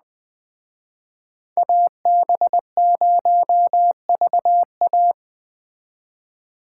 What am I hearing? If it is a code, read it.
Morse code, heard as a keyed tone, about 20 wpm, 700 Hz. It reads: AB0VA
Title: Speed Racing - USA Call Signs 17wpm